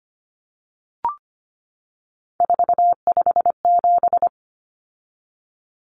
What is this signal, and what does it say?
Morse code, heard as a keyed tone, about 25 wpm, 700 Hz. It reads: 457
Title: Common Cut Numbers - 3-Digit 25wpm